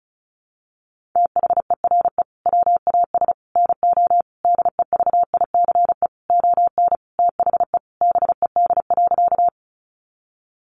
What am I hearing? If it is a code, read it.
Morse code, heard as a keyed tone, about 34 wpm, 700 Hz. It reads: THERE WAS NO DEVICE ON THE BED.
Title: Sentences from Top 1500 Words 35wpm